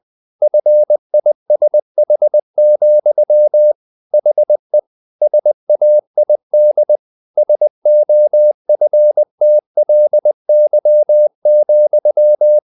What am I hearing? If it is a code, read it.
Morse code, heard as a keyed tone, about 20 wpm, 600 Hz. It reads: FISH, HE SAID SOFTLY,
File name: the old man and the sea / the old man and the sea part 1 20wpm